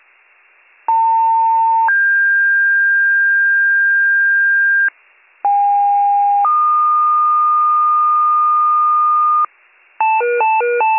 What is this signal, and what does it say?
Two-tone page, heard as a one-second tone, then a three-second tone, about 900 then 1600 Hz, about 810 then 1220 Hz, with noise. It is heard from 0.9 to 9.5 s.